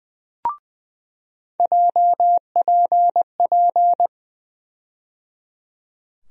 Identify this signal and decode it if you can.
Morse code, heard as a keyed tone, about 20 wpm, 700 Hz. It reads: JPP
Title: USA Call Sign Postfixes - 3 Letters 20wpm